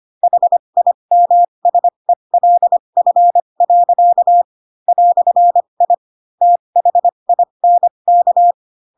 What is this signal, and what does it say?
Morse code, heard as a keyed tone, about 25 wpm, spaced as for 22 wpm, 700 Hz. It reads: HIMSELF. "I THINK
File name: Pinocchio / Pinocchio ch3 18 25 ik1hge